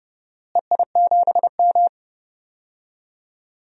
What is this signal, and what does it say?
Morse code, heard as a keyed tone, about 30 wpm, 700 Hz. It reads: EI7M
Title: DXCC-100 Call Signs - 4 Characters - Repeated 2x - 30wpm